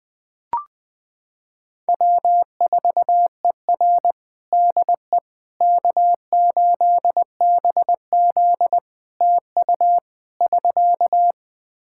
Morse code, heard as a keyed tone, about 20 wpm, 700 Hz. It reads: W4ER DE K8BZ TU <SK>
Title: Practice QSO - Exchange 4 - Ending - 20wpm